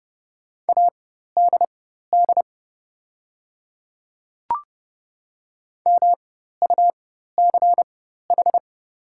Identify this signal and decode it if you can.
Morse code, heard as a keyed tone, about 30 wpm, spaced as for 14 wpm, 700 Hz. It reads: ADD MUCH
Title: 4x Character Spacing - Top 200 Words - 30wpm